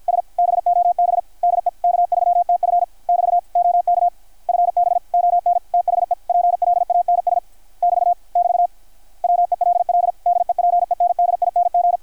Morse code, heard as a keyed tone, about 53 wpm, 700 Hz. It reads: S BOB DE K3TF <BT> OK FB ON THE PLANS <BT> <BT> WELL DEPENDING
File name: archived / CRSnet